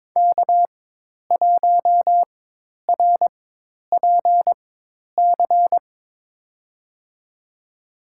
Morse code, heard as a keyed tone, about 21 wpm, spaced as for 10 wpm, 700 Hz. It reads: K1RPC
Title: USA Call Signs - 4x Character Spacing - 22wpm